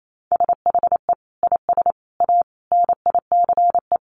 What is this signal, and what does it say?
Morse code, heard as a keyed tone, about 28 wpm, 700 Hz. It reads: SHE IS A NICE
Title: Sentences from Top 1100 Words 28wpm